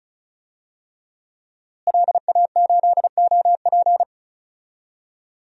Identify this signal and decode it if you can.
Morse code, heard as a keyed tone, about 34 wpm, 700 Hz. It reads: LA8OP
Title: DXCC-100 Call Signs - 5 Characters - Repeated 3x - 35wpm